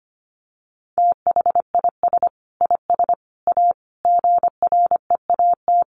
Morse code, heard as a keyed tone, about 25 wpm, 700 Hz. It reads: THIS IS A GREAT
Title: Sentences from Top 1300 Words 25wpm